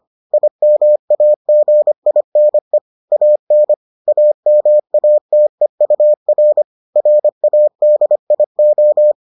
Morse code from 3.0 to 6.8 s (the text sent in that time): AN AMATEUR